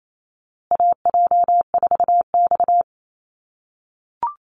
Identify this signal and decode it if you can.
Morse code, heard as a keyed tone, about 28 wpm, 700 Hz. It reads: AJ4X
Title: USA Call Signs Repeated 2x 28wpm